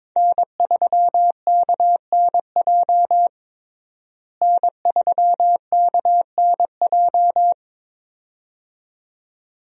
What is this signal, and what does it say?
Morse code, heard as a keyed tone, about 22 wpm, 700 Hz. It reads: N3KNJ N3KNJ
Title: USA Call Signs - 5 Characters - Repeated 2x - 22wpm